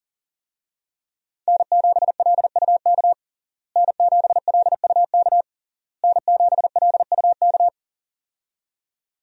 Morse code, heard as a keyed tone, about 40 wpm, 700 Hz. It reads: N7LUK N7LUK N7LUK
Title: USA Call Signs - 5 Characters - Repeated 3x - 40wpm